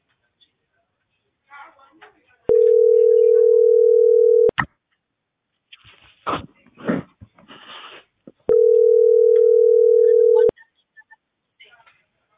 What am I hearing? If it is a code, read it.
Ringback tone (North America), 440 + 480 Hz, with speech.
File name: outgoing recordings